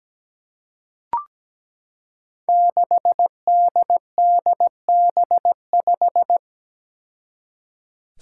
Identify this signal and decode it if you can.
Morse code, heard as a keyed tone, about 17 wpm, 700 Hz. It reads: 6DDB5
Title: Five Challenging Letter-Number Combinations 17wpm